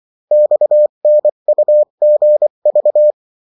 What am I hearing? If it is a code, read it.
Morse code, heard as a keyed tone, about 24 wpm, 600 Hz. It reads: XNUGV
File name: morse mp3 / revision9-vitesse 22 mots